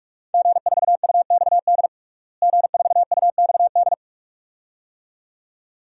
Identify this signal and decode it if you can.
Morse code, heard as a keyed tone, about 46 wpm, 700 Hz. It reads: G4UXD G4UXD
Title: DXCC-100 Call Signs - 5 Characters - Repeated 2x - 45wpm